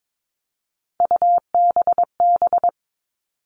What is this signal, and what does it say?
Morse code, heard as a keyed tone, about 22 wpm, 700 Hz. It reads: UBB
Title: All Cut Numbers - 3-Digit 22wpm